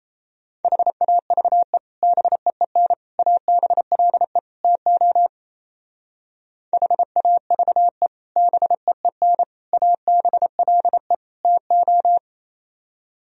Morse code, heard as a keyed tone, about 37 wpm, 700 Hz. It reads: HAVE BEEN ABLE TO HAVE BEEN ABLE TO